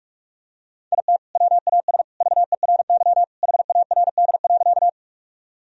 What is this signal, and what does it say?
Morse code, heard as a keyed tone, about 46 wpm, 700 Hz. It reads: IT WAS VERY HARD.
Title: Sentences from Top 1000 Words - Review Part 1 - 45wpm